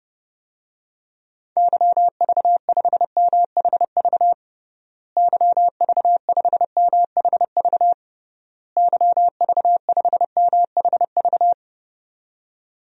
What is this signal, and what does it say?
Morse code, heard as a keyed tone, about 30 wpm, 700 Hz. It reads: YV5MHV YV5MHV YV5MHV
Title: DXCC-100 Call Signs - 6 Characters - Repeated 3x - 30wpm